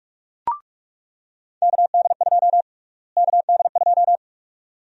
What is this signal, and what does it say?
Morse code, heard as a keyed tone, about 43 wpm, 700 Hz. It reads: KD2 KD2
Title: USA Call Sign Prefixes - Repeated 2x - 45wpm